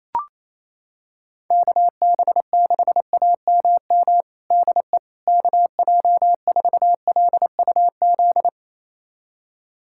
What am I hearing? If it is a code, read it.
Morse code, heard as a keyed tone, about 27 wpm, 700 Hz. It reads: KB6AMM DE KJ4LUZ
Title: USA Callsign DE Callsign - 6 Characters - 28wpm